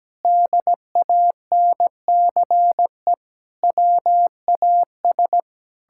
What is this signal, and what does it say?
Morse code, heard as a keyed tone, about 17 wpm, 700 Hz. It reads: DANCE WAS
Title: Sentences from Top 1000 Words - Review Part 6 - 17wpm